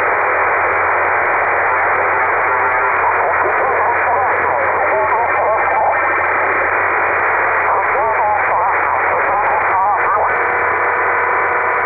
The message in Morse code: none